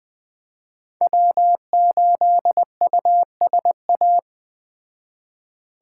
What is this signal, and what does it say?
Morse code, heard as a keyed tone, about 20 wpm, 700 Hz. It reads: W8USA
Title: Speed Racing - USA 1x3 Call Signs 17wpm